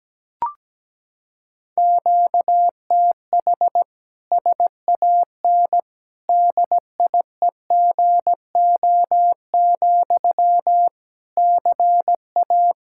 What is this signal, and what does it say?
Morse code, heard as a keyed tone, about 17 wpm, 700 Hz. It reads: QTH SAN DIEGO, CA
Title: Practice QSO - Exchange 1 - Beginning - 17wpm